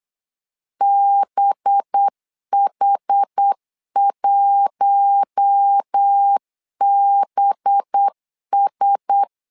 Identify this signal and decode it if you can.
Morse code, heard as a keyed tone, about 8 wpm, 800 Hz. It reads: BH1BS
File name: LEZIONI / 5 LBVC / 5 LB 8wpm 2816